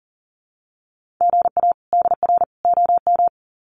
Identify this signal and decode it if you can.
Morse code, heard as a keyed tone, about 40 wpm, 700 Hz. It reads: GA DR OM
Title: Common QSO Phrases 40wpm